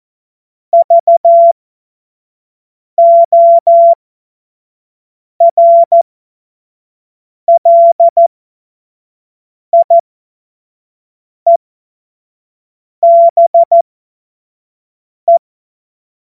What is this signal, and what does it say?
Morse code, heard as a keyed tone, about 14 wpm, spaced as for 5 wpm, 680 Hz. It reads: VORLIEBE
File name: Morsekurs / Witze / Witz 27. April 2021, 25 F70